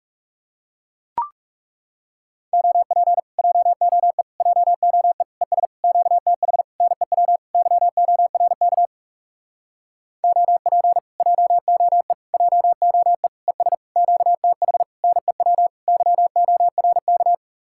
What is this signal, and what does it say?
Morse code, heard as a keyed tone, about 40 wpm, 700 Hz. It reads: OP JOE JOE ES QTH NEW YORK OP JOE JOE ES QTH NEW YORK